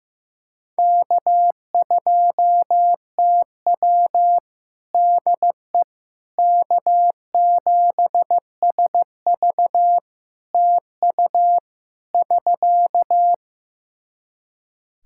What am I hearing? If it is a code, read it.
Morse code, heard as a keyed tone, about 16 wpm, 700 Hz. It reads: K2TW DE K7SV TU <SK>
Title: K1USN SST QSO Practice 15wpm